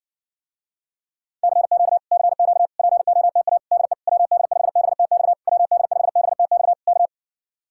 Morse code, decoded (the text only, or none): CQ CQ POTA DE WD5BTX WD5BTX K